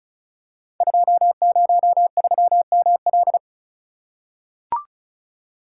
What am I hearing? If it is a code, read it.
Morse code, heard as a keyed tone, about 36 wpm, 700 Hz. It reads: 203ML
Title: Code Groups - 5 Letter:Numbers - 35wpm